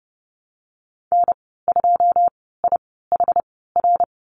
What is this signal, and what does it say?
Morse code, heard as a keyed tone, about 30 wpm, spaced as for 17 wpm, 700 Hz. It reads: N2IHR
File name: Lesson 14g - Number 2 - Call Signs 30wpm